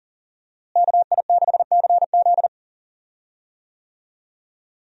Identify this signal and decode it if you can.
Morse code, heard as a keyed tone, about 40 wpm, 700 Hz. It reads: KI6CZ
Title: USA Call Signs - 5 Characters - Repeated 2x - 40wpm